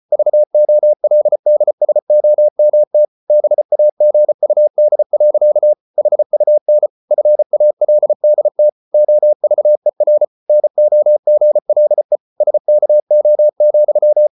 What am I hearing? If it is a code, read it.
Morse code, heard as a keyed tone, about 34 wpm, 600 Hz. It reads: VOLDSOMT BAGUD. HUN FALDT OVER NOGLE SKO,
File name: Ferie QRQ 34wpm